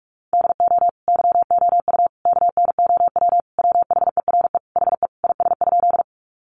Morse code, heard as a keyed tone, about 44 wpm, 700 Hz. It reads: DO YOU KNOW WHERE HE IS?
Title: Sentences from Top 300 Words 45wpm